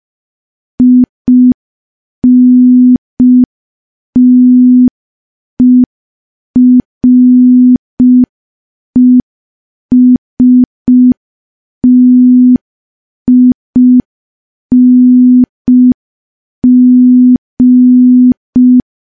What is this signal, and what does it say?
Morse code, heard as a keyed tone, about 5 wpm, 250 Hz. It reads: INTERESTING
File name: morse / morse-training-05WPM